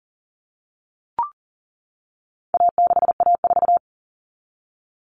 Morse code, heard as a keyed tone, about 40 wpm, 700 Hz. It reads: A6A4
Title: All Cut Numbers - 4-Digit 40wpm